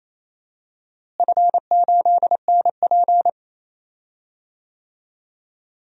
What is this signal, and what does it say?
Morse code, heard as a keyed tone, about 27 wpm, 700 Hz. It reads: F8NP
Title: DXCC-100 Call Signs - 4 Characters - Repeated 3x - 28wpm